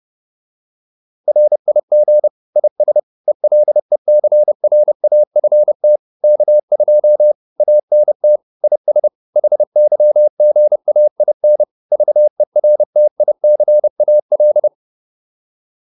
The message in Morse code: RIG IS ELECRAFT K2 ANT IS HYGAIN VERTICAL

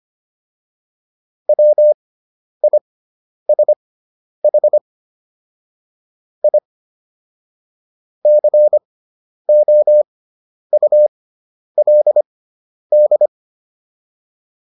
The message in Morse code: WISH I COULD

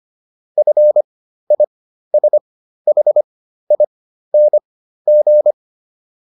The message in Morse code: FISHING

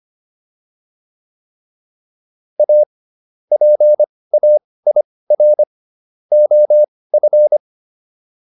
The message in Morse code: A PAIR OF

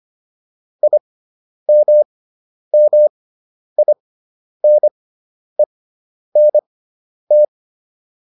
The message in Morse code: IMMINENT

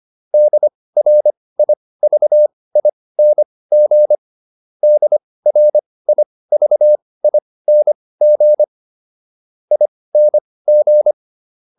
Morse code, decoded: DRIVING DRIVING ING

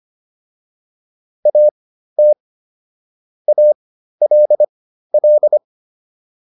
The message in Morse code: AT ALL

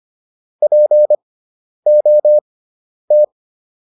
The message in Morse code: POT

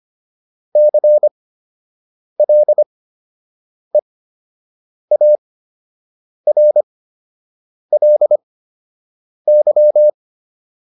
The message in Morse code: CLEARLY

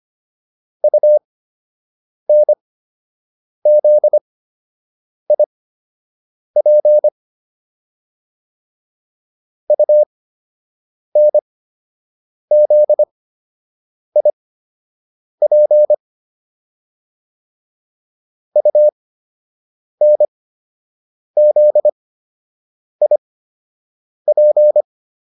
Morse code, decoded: UNZIP UNZIP UNZIP

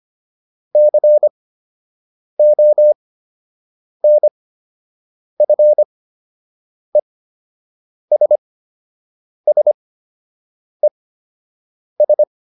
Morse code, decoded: CONFESSES

